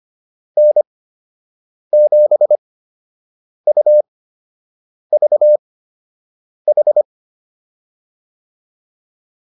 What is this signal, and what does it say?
Morse code, heard as a keyed tone, about 25 wpm, spaced as for 7 wpm, 600 Hz. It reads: N7UVH